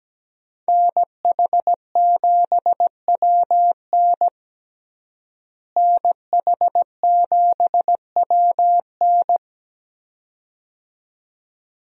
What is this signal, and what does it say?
Morse code, heard as a keyed tone, about 17 wpm, 700 Hz. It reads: NH7WN NH7WN
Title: USA Call Signs - 5 Characters - Repeated 2x - 17wpm